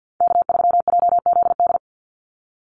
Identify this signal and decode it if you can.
Morse code, heard as a keyed tone, about 50 wpm, 700 Hz. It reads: K3JZD
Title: NA SOTA Chaser Callsigns 50wpm